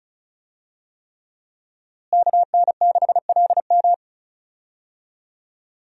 Morse code, heard as a keyed tone, about 34 wpm, 700 Hz. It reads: KN6LM
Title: USA Call Signs - 5 Characters - Repeated 3x - 35wpm